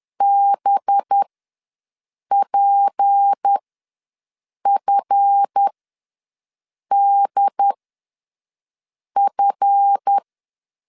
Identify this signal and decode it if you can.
Morse code, heard as a keyed tone, about 10 wpm, spaced as for 5 wpm, 800 Hz. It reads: BPFDF
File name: LEZIONI / 7 GF9Z / 7 GF ws 10wpm 3918